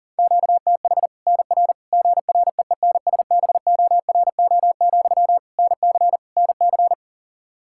Morse code, heard as a keyed tone, about 40 wpm, 700 Hz. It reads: QTH NR GREENSBORO, NC NC
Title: Practice QSO - Exchange 1 - Beginning - 40wpm